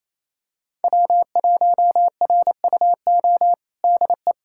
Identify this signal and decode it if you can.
Morse code, heard as a keyed tone, about 28 wpm, 700 Hz. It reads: W1RUO DE
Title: USA Callsign DE Callsign 28wpm